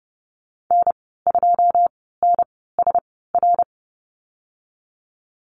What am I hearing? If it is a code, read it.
Morse code, heard as a keyed tone, about 30 wpm, spaced as for 17 wpm, 700 Hz. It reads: N2NSR
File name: Lesson 14g - Number 2 - Call Signs 30wpm